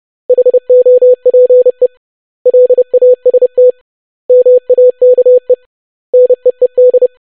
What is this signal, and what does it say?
Morse code, heard as a keyed tone, about 30 wpm, 500 Hz. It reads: HOPE LAST MAKE NEED
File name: basic500-25wpm